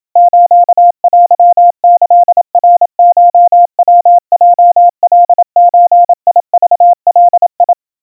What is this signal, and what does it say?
Morse code (Japanese Wabun code), heard as a keyed tone, about 27 wpm, 700 Hz. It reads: ステキナコヤヲカゾクガ
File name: kawa / Morse sample / kamyu 100 cpm 700 Hz Crispy